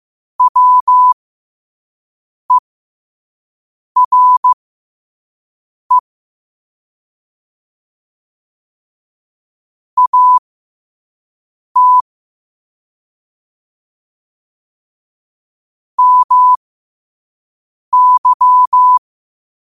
Morse code, heard as a keyed tone, about 15 wpm, spaced as for 5 wpm, 1000 Hz. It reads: WERE AT MY